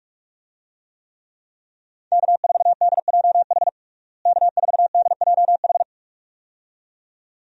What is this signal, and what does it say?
Morse code, heard as a keyed tone, about 45 wpm, 700 Hz. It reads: K4DJH K4DJH
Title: USA Call Signs - 5 Characters - Repeated 2x - 45wpm